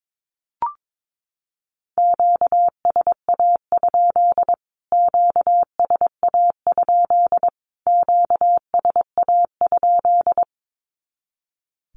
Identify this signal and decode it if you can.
Morse code, heard as a keyed tone, about 22 wpm, 700 Hz. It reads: QSA? QSA? QSA?
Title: Q-Codes Repeated 3x 22wpm